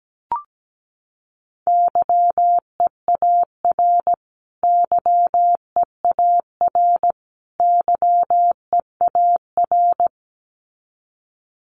Morse code, heard as a keyed tone, about 17 wpm, 700 Hz. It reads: YEAR YEAR YEAR
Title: Top 500 Words Repeated 17wpm